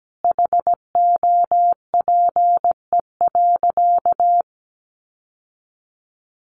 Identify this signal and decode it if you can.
Morse code, heard as a keyed tone, about 17 wpm, 700 Hz. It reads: HOPE.
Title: Sentences from Top 1200 Words 17wpm